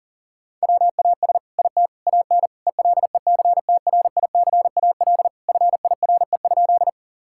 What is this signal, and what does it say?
Morse code, heard as a keyed tone, about 37 wpm, 700 Hz. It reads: WAS IT AN ELECTRICAL FIRE?
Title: Sentences from Top 1000 Words - Review Part 6 - 40wpm